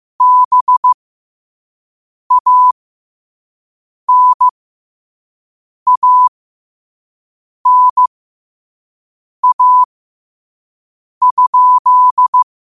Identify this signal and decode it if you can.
Morse code, heard as a keyed tone, about 15 wpm, spaced as for 5 wpm, 1000 Hz. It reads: BANANA?
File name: qotdfiles / 5wpm